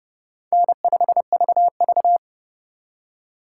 Morse code, heard as a keyed tone, about 36 wpm, 700 Hz. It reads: N5VV
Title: Speed Racing - USA 1x2 Call Signs 30wpm